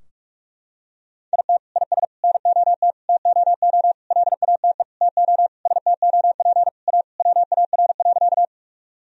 Morse code, heard as a keyed tone, about 46 wpm, 700 Hz. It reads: IT IS NOT TOO LATE TO STOP A WAR.
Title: Sentences from Top 1000 Words - Review Part 2 - 45wpm